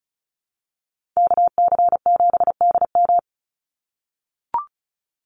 Morse code, heard as a keyed tone, about 34 wpm, 700 Hz. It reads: KC7DM
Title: NA SOTA Activator Callsigns 35wpm